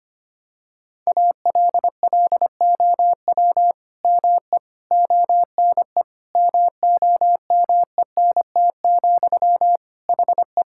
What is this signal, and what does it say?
Morse code, heard as a keyed tone, about 25 wpm, 700 Hz. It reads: ALLOW ME ONE MOMENT, HE
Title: Sentences from Top 1000 Words - Review Part 5 - 25wpm